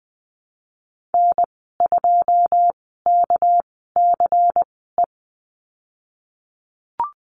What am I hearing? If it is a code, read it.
Morse code, heard as a keyed tone, about 20 wpm, spaced as for 15 wpm, 700 Hz. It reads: N2KCE
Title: 207 Lesson 33g - Letter j - Call Signs 20wpm